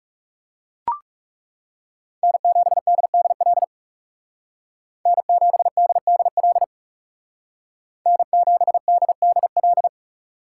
Morse code, heard as a keyed tone, about 38 wpm, 700 Hz. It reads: N7DDL N7DDL N7DDL